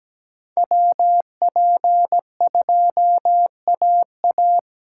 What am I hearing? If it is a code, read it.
Morse code, heard as a keyed tone, about 17 wpm, 700 Hz. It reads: WP2AA
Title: USA Callsign DE Callsign - 5 Characters - 17wpm